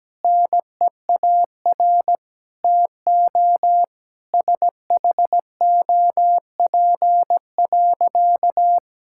Morse code, heard as a keyed tone, about 17 wpm, 700 Hz. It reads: NEAR TO SHOP.
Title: Sentences from Top 1600 Words 17wpm